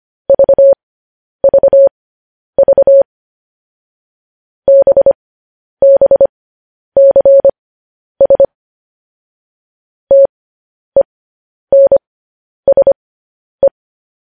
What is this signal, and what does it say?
Morse code, heard as a keyed tone, about 25 wpm, spaced as for 10 wpm, 550 Hz. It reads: VVV BBCS TENSE
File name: podcast / News headlines-1x-25x10